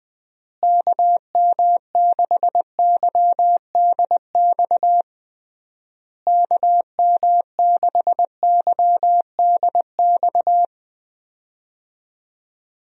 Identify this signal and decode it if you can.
Morse code, heard as a keyed tone, about 21 wpm, 700 Hz. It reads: KM6YDX KM6YDX
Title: USA Call Signs - 6 Characters - Repeated 3x - 20wpm